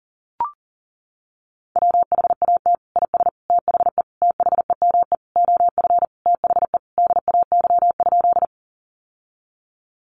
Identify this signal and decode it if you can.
Morse code, heard as a keyed tone, about 40 wpm, 700 Hz. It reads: WHAT IS THE THEME OF THE DAY?
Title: Sentences from Top 1100 Words 40wpm